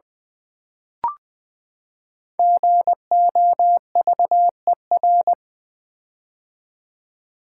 Morse code, heard as a keyed tone, about 20 wpm, 700 Hz. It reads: GOVER
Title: Build A Word - 5 Letters At A Time - 10-Letter Words - 20wpm